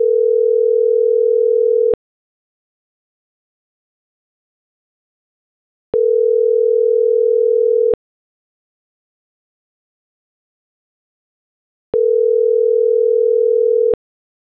Ringback tone (North America), 440 + 480 Hz.
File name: CENTRAL MAERSK / out adolfo.msk